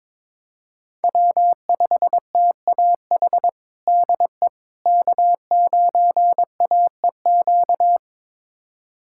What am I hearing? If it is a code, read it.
Morse code, heard as a keyed tone, about 22 wpm, 700 Hz. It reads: W5TAH DE K9AEQ
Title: USA Callsign DE Callsign - 5 Characters - 22wpm